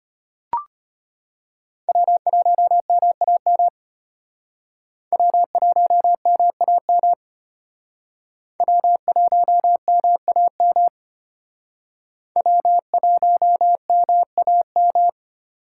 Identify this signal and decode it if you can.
Morse code, heard as a keyed tone, about 33 wpm, 700 Hz. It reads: W1MAM W1MAM W1MAM W1MAM